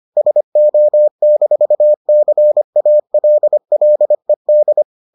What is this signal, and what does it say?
Morse code, heard as a keyed tone, about 25 wpm, 600 Hz. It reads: SO-CALLED